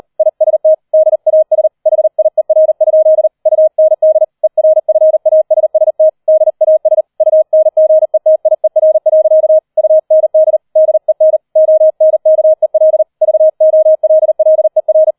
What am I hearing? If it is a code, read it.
Morse code, heard as a keyed tone, about 38 wpm, 630 Hz. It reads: IST DAS HIER? UND ERFASST DAS UNGETIER. UND DEN ONKEL VOLLER
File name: 40wpm